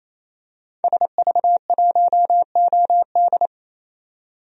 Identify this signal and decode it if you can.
Morse code, heard as a keyed tone, about 28 wpm, 700 Hz. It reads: SV1OD
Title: DXCC-100 Call Signs - 5 Characters - 28wpm